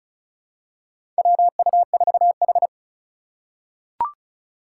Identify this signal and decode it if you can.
Morse code, heard as a keyed tone, about 34 wpm, 690 Hz. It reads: WU4H
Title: USA Call Signs - 4 Characters - 2x Character Spacing - 35wpm